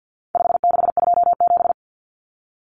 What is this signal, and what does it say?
Morse code, heard as a keyed tone, about 50 wpm, 700 Hz. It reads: 5627
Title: Common Cut Numbers - 4-Digit 50wpm